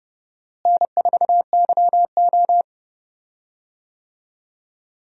Morse code, heard as a keyed tone, about 30 wpm, 700 Hz. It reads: N4YO
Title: USA Call Signs - 4 Characters - Repeated 3x - 30wpm